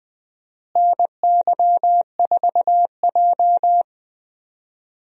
Morse code, heard as a keyed tone, about 21 wpm, 700 Hz. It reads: NY4J
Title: USA Call Signs - 4 Characters - Repeated 3x - 20wpm